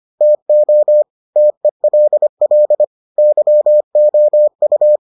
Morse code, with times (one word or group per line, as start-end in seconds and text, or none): 0.2-1.0 TO
1.4-2.8 TELL
3.2-5.0 YOU